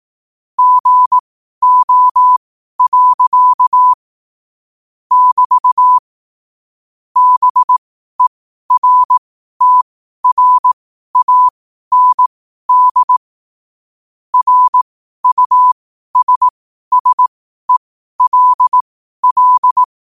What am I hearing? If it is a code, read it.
Morse code, heard as a keyed tone, about 18 wpm, spaced as for 12 wpm, 1000 Hz. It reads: GO. <BT> BERTRAND RUSSELL
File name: qotdfiles / 12wpm